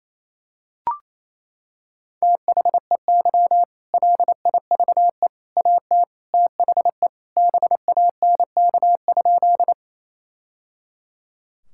Morse code, heard as a keyed tone, about 28 wpm, 700 Hz. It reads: THEY LIVE AT THE BANK?
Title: Sentences from Top 1000 Words - Review Part 6 - 28wpm